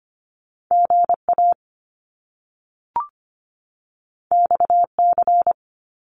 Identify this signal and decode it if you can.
Morse code, heard as a keyed tone, about 25 wpm, 700 Hz. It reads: GA XC
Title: Top Two Letter Combinations in Words 25wpm